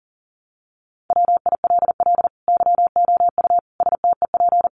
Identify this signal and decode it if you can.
Morse code, heard as a keyed tone, about 40 wpm, 700 Hz. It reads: WILL YOU STEP
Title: Sentences from Top 1300 Words 40wpm